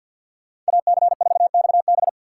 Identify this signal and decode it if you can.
Morse code, heard as a keyed tone, about 64 wpm, 700 Hz. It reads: AC4XB